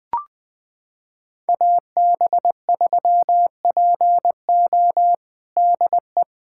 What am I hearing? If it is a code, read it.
Morse code, heard as a keyed tone, about 20 wpm, 700 Hz. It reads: AB3PO DE
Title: USA Callsign DE Callsign - 5 Characters - 20wpm